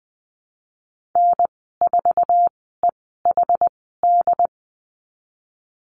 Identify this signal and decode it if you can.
Morse code, heard as a keyed tone, about 20 wpm, spaced as for 15 wpm, 700 Hz. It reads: N4EHD
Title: 076 Lesson 13g - Letter L - Call Signs 20wpm